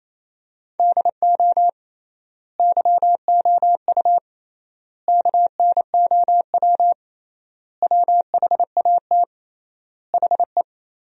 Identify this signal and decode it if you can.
Morse code, heard as a keyed tone, about 28 wpm, 700 Hz. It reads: DO YOU KNOW WHAT HE
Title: 3x Word-Spacing - Sentences from Top 300 Words - 28wpm